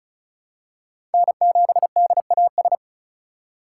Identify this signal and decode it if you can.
Morse code, heard as a keyed tone, about 36 wpm, 700 Hz. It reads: N7DAS
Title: USA Call Signs - 2x Character Spacing - 35wpm